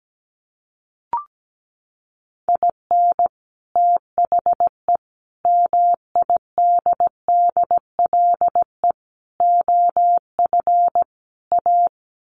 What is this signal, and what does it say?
Morse code, heard as a keyed tone, about 17 wpm, 700 Hz. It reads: IN THE MIDDLE OF A
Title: Sets of 5 Words 17wpm